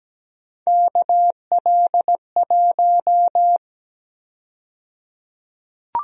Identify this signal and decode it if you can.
Morse code, heard as a keyed tone, about 17 wpm, 700 Hz. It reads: KL1
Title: USA Call Sign Prefixes - 17wpm